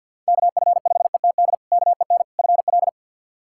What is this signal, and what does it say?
Morse code, heard as a keyed tone, about 50 wpm, 700 Hz. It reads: KF5ETD KEN FL
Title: K1USN SST QSO Practice - 50wpm